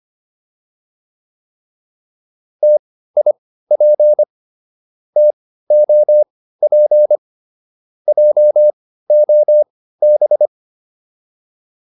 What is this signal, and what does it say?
Morse code, heard as a keyed tone, about 25 wpm, spaced as for 15 wpm, 600 Hz. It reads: TIP TOP JOB